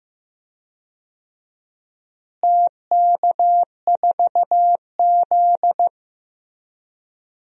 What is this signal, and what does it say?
Morse code, heard as a keyed tone, about 15 wpm, 700 Hz. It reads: TK4Z
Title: DXCC-100 Call Signs - 4 Characters - Repeated 2x - 15wpm